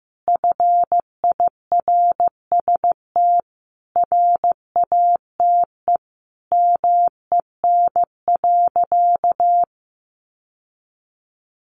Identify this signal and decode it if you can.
Morse code, heard as a keyed tone, about 14 wpm, 700 Hz. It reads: FIRST RATE MEN.
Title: Sentences from Top 1300 Words 15wpm